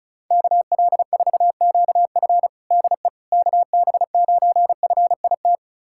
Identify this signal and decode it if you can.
Morse code, heard as a keyed tone, about 34 wpm, 700 Hz. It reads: KL4QF DE KB9FIT
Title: USA Callsign DE Callsign 35wpm